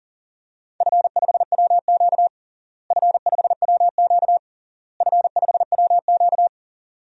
Morse code, heard as a keyed tone, about 40 wpm, 700 Hz. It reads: F5WQ F5WQ F5WQ
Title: DXCC-100 Call Signs - 4 Characters - Repeated 3x - 40wpm